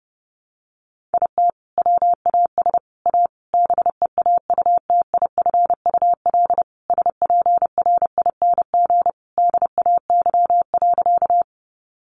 Morse code, heard as a keyed tone, about 30 wpm, 700 Hz. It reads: IT WAS A BEAUTIFUL SPRING DAY.
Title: Sentences from Top 1200 Words 30wpm